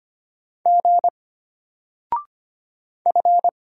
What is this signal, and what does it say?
Morse code, heard as a keyed tone, about 25 wpm, 700 Hz. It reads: G F
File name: Single-Letter-Number-25wpm